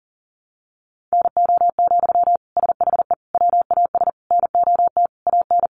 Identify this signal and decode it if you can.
Morse code, heard as a keyed tone, about 40 wpm, 700 Hz. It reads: NO, SHE WAS NOT AN
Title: Sentences from Top 800 Words 40wpm